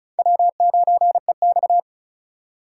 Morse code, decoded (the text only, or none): W9EX